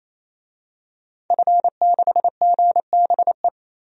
Morse code, heard as a keyed tone, about 28 wpm, 700 Hz. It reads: F6GBE
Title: DXCC-100 Call Signs - 5 Characters - 28wpm